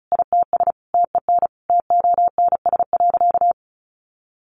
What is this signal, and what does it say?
Morse code, heard as a keyed tone, about 34 wpm, 700 Hz. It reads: ITS TEN TONS.
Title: A Sound Of Thunder 35wpm